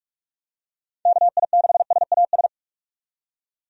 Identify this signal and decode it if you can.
Morse code, heard as a keyed tone, about 41 wpm, 700 Hz. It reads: KI6SAS
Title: USA Call Signs - 6 Characters - 45wpm